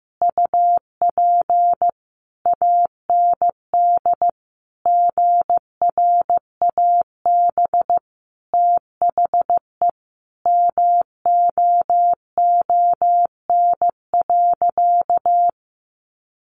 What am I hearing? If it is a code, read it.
Morse code, heard as a keyed tone, about 15 wpm, 700 Hz. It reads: UP AND GRAB THE MOON.
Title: A Sound Of Thunder 15wpm